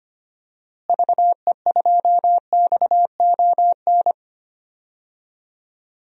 Morse code, heard as a keyed tone, about 25 wpm, 700 Hz. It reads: VE2XON
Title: DXCC-100 Call Signs - 6 Characters - Repeated 2x - 25wpm